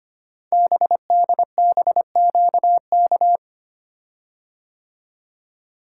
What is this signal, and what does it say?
Morse code, heard as a keyed tone, about 25 wpm, 700 Hz. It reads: BDBQK
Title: Code Groups - 5 Letters - 25wpm